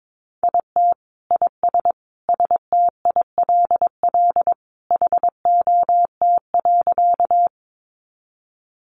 Morse code, heard as a keyed tone, about 22 wpm, 700 Hz. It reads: IT IS STILL HOT.
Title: Easy Phrases 22wpm